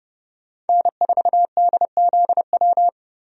Morse code, heard as a keyed tone, about 30 wpm, 700 Hz. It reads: N4DZW
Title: USA Call Signs - 5 Characters - 30wpm